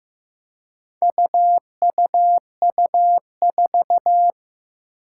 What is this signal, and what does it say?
Morse code, heard as a keyed tone, about 15 wpm, 700 Hz. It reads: UUU4
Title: Four Challenging Letter-Number Combinations 15wpm